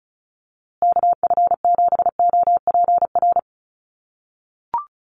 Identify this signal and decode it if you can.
Morse code, heard as a keyed tone, about 34 wpm, 700 Hz. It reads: KF7OPR
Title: USA Call Signs Repeated 2x 35wpm